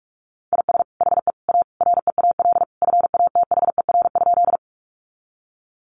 Morse code, heard as a keyed tone, about 44 wpm, 700 Hz. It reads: IS HE A REAL FATHER?
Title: Sentences from Top 300 Words 45wpm